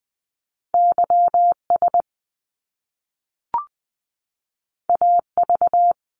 Morse code, heard as a keyed tone, about 20 wpm, 700 Hz. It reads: YS AV
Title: Top Two Letter Combinations in Words 20wpm